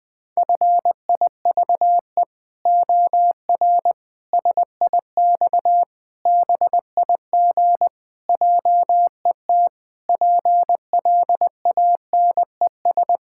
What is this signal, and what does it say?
Morse code, heard as a keyed tone, about 20 wpm, 700 Hz. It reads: FIVE OR SIX BIG JET PLANES
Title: Pangram Challenge v2 - 20wpm